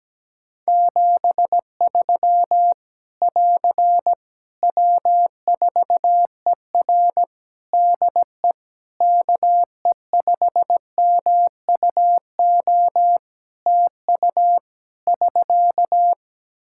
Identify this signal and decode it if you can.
Morse code, heard as a keyed tone, about 17 wpm, 700 Hz. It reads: 73 <AR> W4ER DE KE5MUO TU <SK>
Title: K1USN SST QSO Practice 17wpm